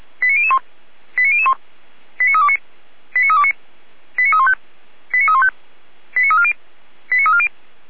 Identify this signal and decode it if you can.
Five-tone selective call (ZVEI), heard as quick runs of five tones, tones of about 70 ms. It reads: 89001 89001 89329 89329 89315 89315 89340 89340